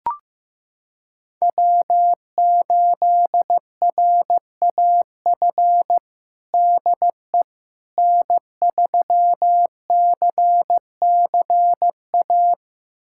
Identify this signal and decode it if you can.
Morse code, heard as a keyed tone, about 15 wpm, 700 Hz. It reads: W8RAF DE N3CCA
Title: USA Callsign DE Callsign - 5 Characters - 15wpm